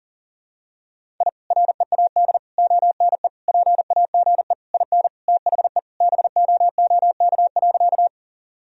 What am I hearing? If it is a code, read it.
Morse code, heard as a keyed tone, about 40 wpm, 700 Hz. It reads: I READ ONE PAGE IN THE BOOK.
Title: Sentences from Top 1000 Words - Review Part 1 - 40wpm